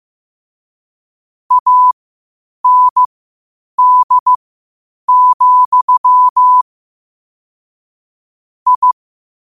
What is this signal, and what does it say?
Morse code, heard as a keyed tone, about 15 wpm, spaced as for 8 wpm, 1000 Hz. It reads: AND, I
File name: qotdfiles / 8wpm